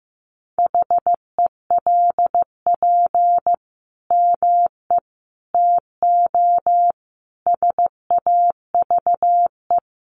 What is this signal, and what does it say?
Morse code, heard as a keyed tone, about 15 wpm, 700 Hz. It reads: HELP ME TO SAVE
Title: Sentences from Top 1100 Words 15wpm